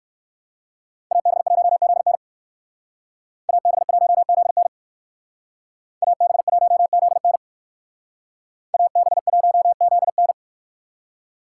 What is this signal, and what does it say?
Morse code, heard as a keyed tone, about 60 wpm, 700 Hz. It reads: AB1ZN AB1ZN AB1ZN AB1ZN